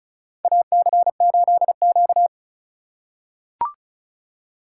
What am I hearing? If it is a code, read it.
Morse code, heard as a keyed tone, about 36 wpm, 700 Hz. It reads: AC8Q
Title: USA Call Signs - 4 Characters - 3x Character Spacing - 35wpm